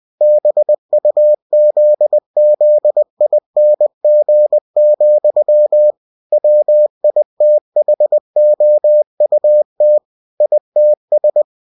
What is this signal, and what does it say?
Morse code, heard as a keyed tone, about 20 wpm, 600 Hz. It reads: BUZZING, WITHOUT ITS
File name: cw-books / winnie-the-pooh / winnie-20wpm-page-0002